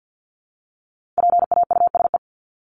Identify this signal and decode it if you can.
Morse code, heard as a keyed tone, about 50 wpm, 700 Hz. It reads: PAUSE
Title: Top 1501-1750 Words 50wpm